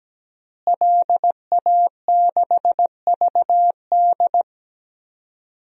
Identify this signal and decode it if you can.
Morse code, heard as a keyed tone, about 17 wpm, 700 Hz. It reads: LA6VD
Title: DXCC-100 Call Signs - 5 Characters - Repeated 3x - 17wpm